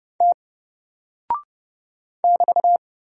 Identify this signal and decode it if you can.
Morse code, heard as a keyed tone, about 30 wpm, 700 Hz. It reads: T <BT>
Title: Single Character (Letter:Number:Punctuation:Prosign) 30wpm